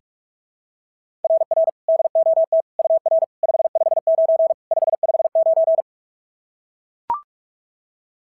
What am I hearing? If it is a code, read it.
Morse code, heard as a keyed tone, about 44 wpm, 690 Hz. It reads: RR DOT UR 559 559
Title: SOTA QSO Practice 45wpm